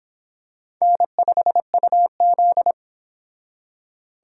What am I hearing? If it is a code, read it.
Morse code, heard as a keyed tone, about 25 wpm, 700 Hz. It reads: N5UZ